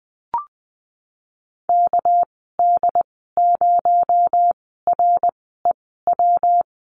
Morse code, heard as a keyed tone, about 19 wpm, spaced as for 14 wpm, 700 Hz. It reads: KD0REW
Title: 231 Lesson 37g - Letter x - Call Signs 20wpm